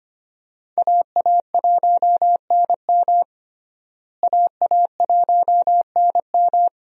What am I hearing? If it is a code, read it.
Morse code, heard as a keyed tone, about 25 wpm, 700 Hz. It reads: AA1NM AA1NM
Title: USA Call Signs - 5 Characters - Repeated 3x - 25wpm